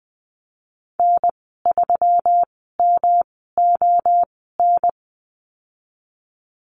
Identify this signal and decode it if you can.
Morse code, heard as a keyed tone, about 20 wpm, spaced as for 15 wpm, 700 Hz. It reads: N3MON
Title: 179 Lesson 29g - Slash - Call Signs 20wpm